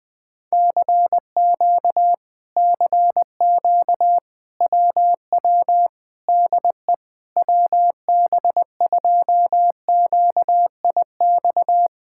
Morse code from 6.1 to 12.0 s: DE WB2QIX

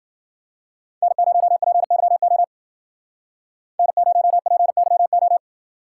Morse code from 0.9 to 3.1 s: N0PYO

